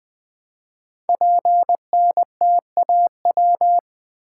Morse code, heard as a keyed tone, about 20 wpm, 700 Hz. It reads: PNTAW
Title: Code Groups - 5 Letters - 20wpm